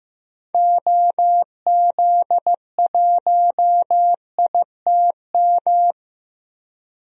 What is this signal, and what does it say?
Morse code, heard as a keyed tone, about 15 wpm, 690 Hz. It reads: OZ1ITM
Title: DXCC-100 Call Signs - 6 Characters - 15wpm